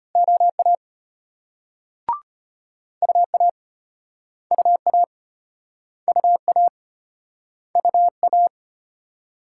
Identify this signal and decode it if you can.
Morse code, heard as a keyed tone, about 36 wpm, 700 Hz. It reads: OA UA UA UA UA